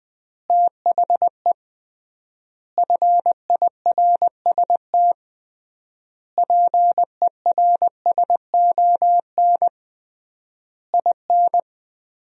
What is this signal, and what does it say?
Morse code, heard as a keyed tone, about 21 wpm, 700 Hz. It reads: THE FIRST PERSON IN
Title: 3x Word-Spacing - Sentences from Top 400 Words - 20wpm